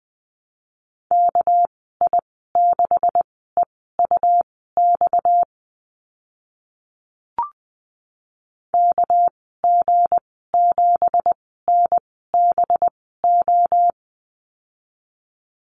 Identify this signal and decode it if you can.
Morse code, heard as a keyed tone, about 20 wpm, spaced as for 14 wpm, 700 Hz. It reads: KI6EUX KG7NBO
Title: 231 Lesson 37g - Letter x - Call Signs 20wpm